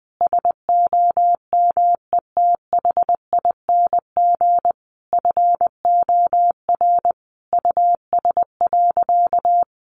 Morse code, heard as a keyed tone, about 20 wpm, 700 Hz. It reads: SOMETHING FOR US.
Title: Sentences from Top 1200 Words 20wpm